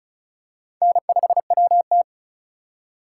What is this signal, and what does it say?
Morse code, heard as a keyed tone, about 36 wpm, 700 Hz. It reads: N5WT
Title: USA Call Signs - 4 Characters - 6x Character Spacing - 35wpm